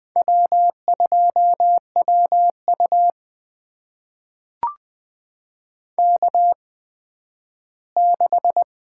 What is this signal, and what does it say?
Morse code, heard as a keyed tone, about 20 wpm, spaced as for 12 wpm, 700 Hz. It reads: W2WU K6
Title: USA Call Signs - 4 Characters - 8x Character Spacing - 20wpm